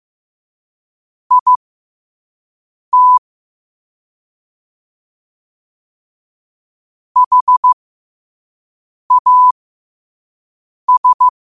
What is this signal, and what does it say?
Morse code, heard as a keyed tone, about 15 wpm, spaced as for 5 wpm, 1000 Hz. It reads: IT HAS